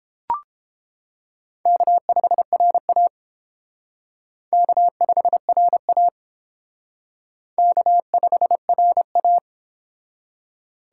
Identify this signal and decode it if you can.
Morse code, heard as a keyed tone, about 29 wpm, 700 Hz. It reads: K5RA K5RA K5RA